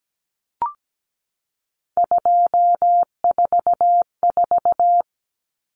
Morse code, heard as a keyed tone, about 17 wpm, 700 Hz. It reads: 244
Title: Numbers - 3-Digit 17wpm